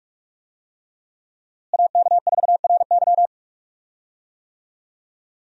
Morse code, heard as a keyed tone, about 46 wpm, 700 Hz. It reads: AK4RY
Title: USA Call Signs - 5 Characters - Repeated 3x - 45wpm